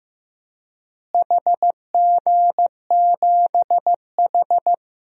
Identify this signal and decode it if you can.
Morse code, heard as a keyed tone, about 16 wpm, 700 Hz. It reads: HG7H
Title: DXCC-100 Call Signs - 4 Characters - Repeated 3x - 15wpm